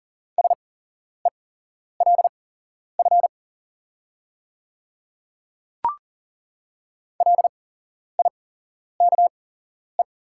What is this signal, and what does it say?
Morse code, heard as a keyed tone, about 40 wpm, spaced as for 11 wpm, 700 Hz. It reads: SELF LIKE
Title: 8x Character Spacing - Top 200 Words - 40wpm